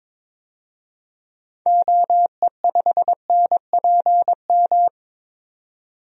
Morse code, heard as a keyed tone, about 22 wpm, 700 Hz. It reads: OE5NPM
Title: DXCC-100 Call Signs - 6 Characters - 22wpm